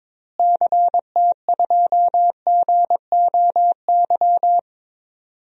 Morse code, heard as a keyed tone, about 22 wpm, 700 Hz. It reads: CT2GOY
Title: DXCC-100 Call Signs - 6 Characters - Repeated 2x - 22wpm